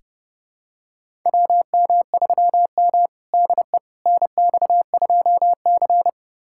Morse code, heard as a keyed tone, about 30 wpm, 700 Hz. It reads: WM3M DE NX2C
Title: USA Callsign DE Callsign - 4 Characters - 30wpm